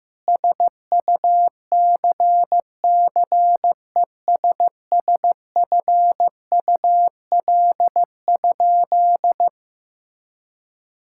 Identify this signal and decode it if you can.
Morse code, heard as a keyed tone, about 16 wpm, 700 Hz. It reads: SUCCESSFUL?
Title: Sentences from Top 1000 Words - Review Part 5 - 15wpm